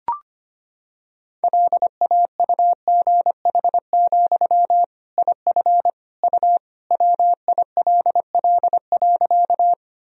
Morse code, heard as a keyed tone, about 25 wpm, 700 Hz. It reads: LAUGH, IF U WILL.
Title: Sentences from Top 500 Words and Abbreviations 25wpm